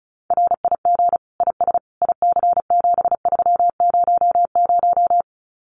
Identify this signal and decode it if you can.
Morse code, heard as a keyed tone, about 34 wpm, 700 Hz. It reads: RIG IS IC7300
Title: Common QSO Phrases Encore 35wpm